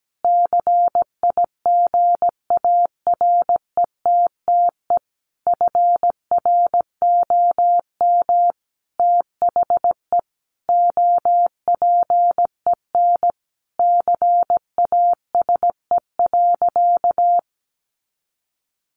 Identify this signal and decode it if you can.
Morse code, heard as a keyed tone, about 17 wpm, 700 Hz. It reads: CIGARETTE FROM THE OPEN CASE.
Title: Sentences from Top 1000 Words 17wpm